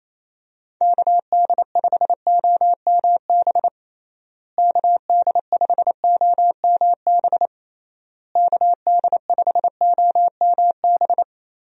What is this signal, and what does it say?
Morse code, heard as a keyed tone, about 28 wpm, 700 Hz. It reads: KD5OMB KD5OMB KD5OMB
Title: USA Call Signs - 6 Characters - Repeated 3x - 28wpm